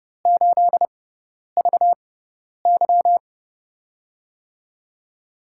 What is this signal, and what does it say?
Morse code, heard as a keyed tone, about 30 wpm, spaced as for 10 wpm, 700 Hz. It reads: 8VY
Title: USA Call Signs - 4 Characters - 6x Character Spacing - 30wpm